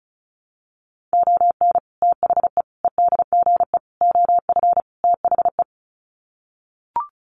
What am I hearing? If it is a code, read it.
Morse code, heard as a keyed tone, about 34 wpm, 700 Hz. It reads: ON THE EDGE OF THE
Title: Sets of 5 Words 35wpm